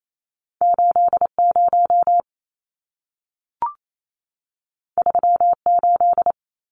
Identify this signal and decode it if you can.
Morse code, heard as a keyed tone, about 28 wpm, 700 Hz. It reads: 80 38
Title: Numbers - 2-Digit 28wpm